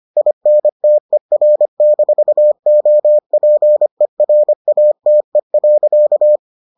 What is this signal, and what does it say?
Morse code, heard as a keyed tone, about 25 wpm, 600 Hz. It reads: INTER-OPERATE.